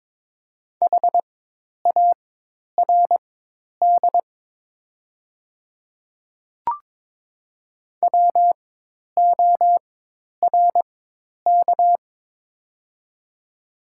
Morse code, heard as a keyed tone, about 22 wpm, spaced as for 10 wpm, 700 Hz. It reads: HARD WORK
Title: 4x Character Spacing - Top 300 Words - 22wpm